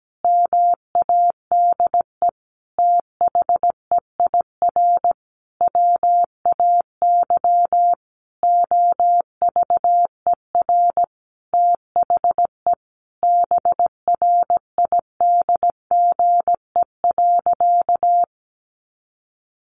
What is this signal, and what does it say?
Morse code, heard as a keyed tone, about 17 wpm, 700 Hz. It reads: MADE THEIR WAY OVER THE BRIDGE.
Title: Sentences from Top 1500 Words 17wpm